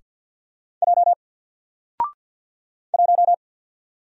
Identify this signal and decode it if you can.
Morse code, heard as a keyed tone, about 48 wpm, 700 Hz. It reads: J 1
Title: Single Letter-Number Alt 50wpm